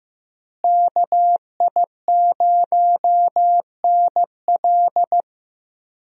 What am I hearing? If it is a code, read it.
Morse code, heard as a keyed tone, about 15 wpm, 700 Hz. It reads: KI0NL
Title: USA Call Signs - 5 Characters - Repeated 3x - 15wpm